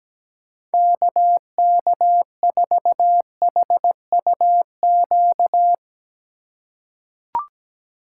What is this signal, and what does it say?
Morse code, heard as a keyed tone, about 17 wpm, 700 Hz. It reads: KK4HUQ
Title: USA Call Signs - 6 Characters - 3x Character Spacing - 17wpm